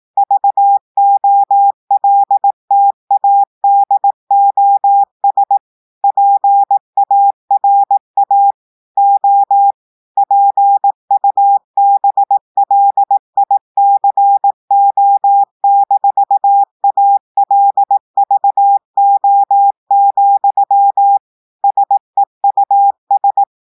Morse code, heard as a keyed tone, about 18 wpm, 800 Hz. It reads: VOLTADOS PARA O PUBLICO-ALVO, SEUS
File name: audios / portugais / portugais texte en clair 2 vitesse 18 mots minutes